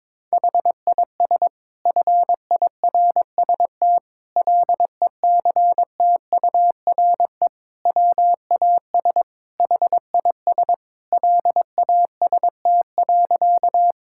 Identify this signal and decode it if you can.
Morse code, heard as a keyed tone, about 22 wpm, 700 Hz. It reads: HIS FIRST LECTURE WAS HIS LAST.
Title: Sentences from Top 1600 Words 22wpm